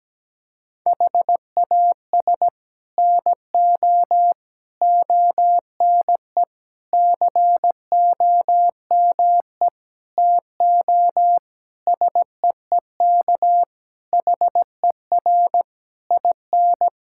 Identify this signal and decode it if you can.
Morse code, heard as a keyed tone, about 17 wpm, 700 Hz. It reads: HAS NO ONE COME TO SEEK HER IN
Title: Sentences from Top 1600 Words 17wpm